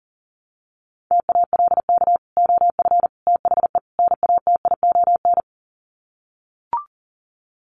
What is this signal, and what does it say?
Morse code, heard as a keyed tone, about 37 wpm, 700 Hz. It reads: TALK OF THE NATION
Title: Sets of 4 Words 40wpm